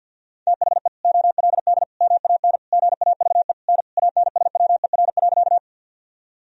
Morse code, heard as a keyed tone, about 46 wpm, 700 Hz. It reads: THE OLD MAN GAVE N ANSWER.
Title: Sentences from Top 500 Words and Abbreviations 50wpm